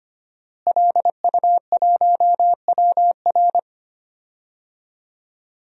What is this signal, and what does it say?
Morse code, heard as a keyed tone, about 25 wpm, 700 Hz. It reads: LU1WR
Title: DXCC-100 Call Signs - 5 Characters - Repeated 3x - 25wpm